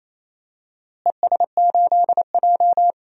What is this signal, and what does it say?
Morse code, heard as a keyed tone, about 27 wpm, 700 Hz. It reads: ES8J
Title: DXCC-100 Call Signs - 4 Characters - Repeated 2x - 28wpm